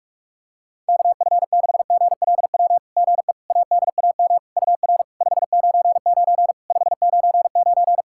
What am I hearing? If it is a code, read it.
Morse code, heard as a keyed tone, about 46 wpm, 690 Hz. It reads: KF6GLW GE ADAM UR 599 599
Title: SOTA QSO Practice 45wpm